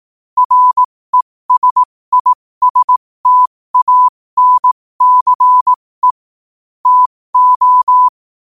Morse code, heard as a keyed tone, about 18 wpm, spaced as for 15 wpm, 1000 Hz. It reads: RESISTANCE TO
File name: qotdfiles / 15wpm